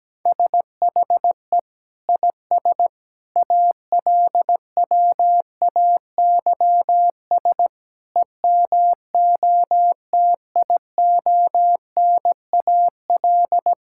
Morse code, heard as a keyed tone, about 17 wpm, 700 Hz. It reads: SHE IS ALWAYS EMOTIONAL
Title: Sentences from Top 1000 Words - Review Part 5 - 17wpm